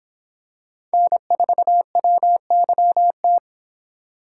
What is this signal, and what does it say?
Morse code, heard as a keyed tone, about 23 wpm, 700 Hz. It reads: N4WYT